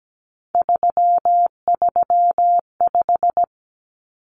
Morse code, heard as a keyed tone, about 17 wpm, 700 Hz. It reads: 335
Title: Numbers - 3-Digit 17wpm